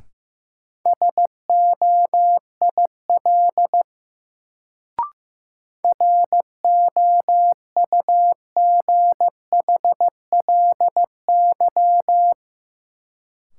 Morse code, heard as a keyed tone, about 15 wpm, 700 Hz. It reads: SOIL ROUGHLY
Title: Sentences from Top 1500 Words - New Word Drill Alt - 15wpm